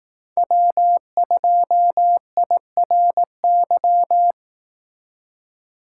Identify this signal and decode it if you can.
Morse code, heard as a keyed tone, about 18 wpm, 700 Hz. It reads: W2IRY
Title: Speed Racing - USA 1x3 Call Signs 15wpm